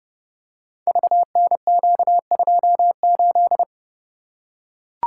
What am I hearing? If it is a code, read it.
Morse code, heard as a keyed tone, about 32 wpm, 700 Hz. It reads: VNQ28
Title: Code Groups - 5 Letter:Numbers - 30wpm